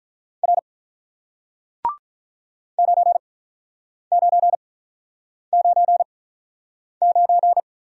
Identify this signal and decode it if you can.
Morse code, heard as a keyed tone, about 48 wpm, spaced as for 9 wpm, 700 Hz. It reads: R 9999